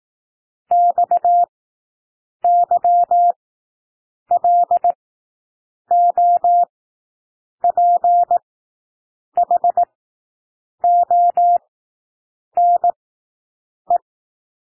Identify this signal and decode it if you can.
Morse code, heard as a keyed tone, about 18 wpm, spaced as for 7 wpm, 700 Hz. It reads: XYLOPHONE
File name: SOUND WORDS 18 wpm